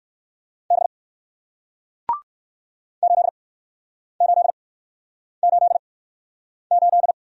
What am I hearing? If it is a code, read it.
Morse code, heard as a keyed tone, about 60 wpm, spaced as for 9 wpm, 700 Hz. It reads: B 8888